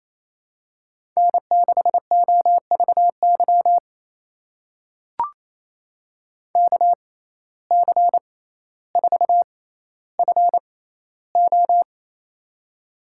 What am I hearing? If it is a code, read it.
Morse code, heard as a keyed tone, about 28 wpm, spaced as for 14 wpm, 700 Hz. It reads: N6OVY KC4FO
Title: USA Call Signs - 6x Character Spacing - 28wpm